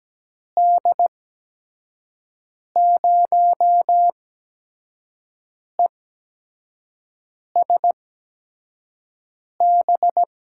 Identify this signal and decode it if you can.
Morse code, heard as a keyed tone, about 17 wpm, spaced as for 5 wpm, 700 Hz. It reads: D0ESB
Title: USA Call Signs - 6 Characters - 8x Character Spacing - 17wpm